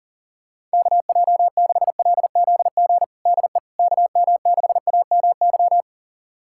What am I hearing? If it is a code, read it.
Morse code, heard as a keyed tone, about 40 wpm, 700 Hz. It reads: KJ6LZG DE KM6AMY
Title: USA Callsign DE Callsign - 6 Characters - 40wpm